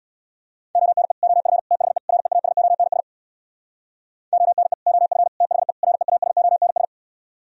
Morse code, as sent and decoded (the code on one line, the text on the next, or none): --- -. / --- ..-. / - .... / .-. .- ... --- -. ... / --- -. / --- ..-. / - .... / .-. .- ... --- -. ...
ONE OF THE REASONS ONE OF THE REASONS